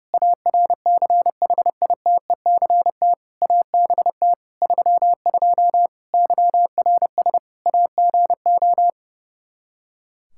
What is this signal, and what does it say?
Morse code, heard as a keyed tone, about 30 wpm, 700 Hz. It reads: ARCHITECT ABT 32 YRS AGO
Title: Practice QSO - Exchange 3 - More Info - 30wpm